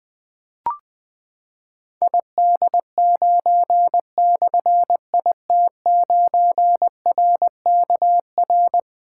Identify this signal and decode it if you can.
Morse code, heard as a keyed tone, about 21 wpm, 700 Hz. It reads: ID9/IT9RKR
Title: Difficult Call Signs 20wpm